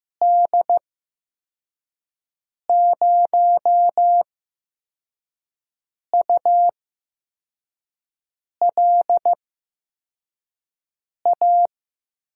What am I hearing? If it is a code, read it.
Morse code, heard as a keyed tone, about 15 wpm, spaced as for 4 wpm, 700 Hz. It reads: D0ULA
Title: USA Call Signs - 6 Characters - 8x Character Spacing - 15wpm